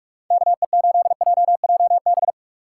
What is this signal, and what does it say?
Morse code, heard as a keyed tone, about 46 wpm, 700 Hz. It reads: KE8JJB
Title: USA Call Signs - 6 Characters - 6x Character Spacing - 45wpm